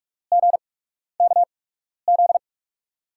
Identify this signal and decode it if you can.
Morse code, heard as a keyed tone, about 47 wpm, spaced as for 12 wpm, 700 Hz. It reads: GKZ
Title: USA Call Signs - 6 Characters - 8x Character Spacing - 45wpm